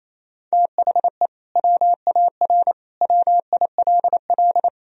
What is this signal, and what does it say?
Morse code, heard as a keyed tone, about 28 wpm, 700 Hz. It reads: THE WAR WILL
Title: Sentences from Top 1000 Words - Review Part 2 - 28wpm